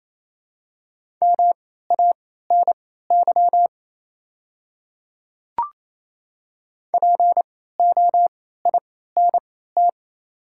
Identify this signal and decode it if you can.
Morse code, heard as a keyed tone, about 28 wpm, spaced as for 16 wpm, 690 Hz. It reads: MANY POINT
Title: 3x Character Spacing - Top 300 Words - 28wpm